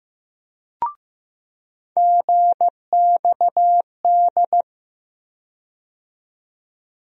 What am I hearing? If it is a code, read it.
Morse code, heard as a keyed tone, about 15 wpm, 700 Hz. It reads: GXD
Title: USA Call Sign Postfixes - 3 Letters 15wpm